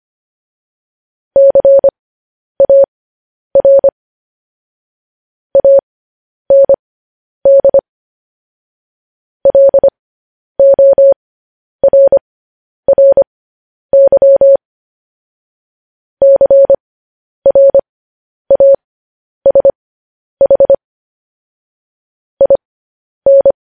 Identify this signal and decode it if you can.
Morse code, heard as a keyed tone, about 25 wpm, spaced as for 10 wpm, 550 Hz. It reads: CAR AND LORRY CRASH IN
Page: news headlines in morse code at 10 wpm